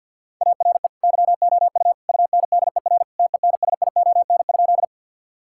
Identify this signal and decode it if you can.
Morse code, heard as a keyed tone, about 50 wpm, 700 Hz. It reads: ARE YOU UNDER TENSION?
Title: Sentences from Top 1600 Words 50wpm